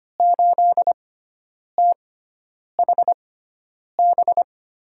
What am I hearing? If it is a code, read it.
Morse code, heard as a keyed tone, about 25 wpm, spaced as for 9 wpm, 700 Hz. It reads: 8THB
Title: USA Call Signs - 6x Character Spacing - 25wpm